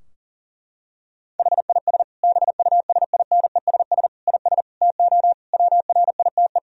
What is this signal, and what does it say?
Morse code, heard as a keyed tone, about 40 wpm, 700 Hz. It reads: HIS BUSINESS IS TO WRITE
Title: Sentences from Top 1000 Words - Review Part 3 - 40wpm